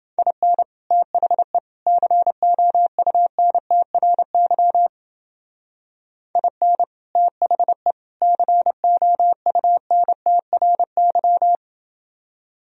Morse code, heard as a keyed tone, about 29 wpm, 700 Hz. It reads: IN THE COUNTRY IN THE COUNTRY